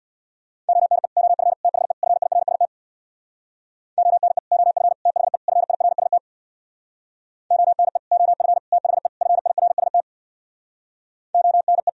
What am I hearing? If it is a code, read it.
Morse code, heard as a keyed tone, about 64 wpm, 700 Hz. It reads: ONE OF THE FIRST ONE OF THE FIRST ONE OF THE FIRST ONE